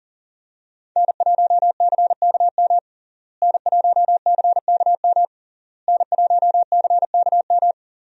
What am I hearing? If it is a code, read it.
Morse code, heard as a keyed tone, about 40 wpm, 700 Hz. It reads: N1CKM N1CKM N1CKM
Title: USA Call Signs - 5 Characters - Repeated 3x - 40wpm